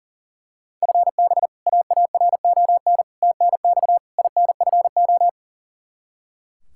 Morse code, heard as a keyed tone, about 40 wpm, 700 Hz. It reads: FB AARON TNX INFO
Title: Practice QSO - Exchange 3 - More Info - 40wpm